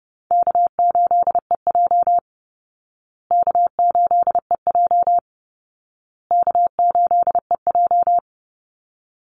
Morse code, heard as a keyed tone, about 30 wpm, 700 Hz. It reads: K8EJ K8EJ K8EJ
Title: Easy Call Signs Repeated 3x - 30wpm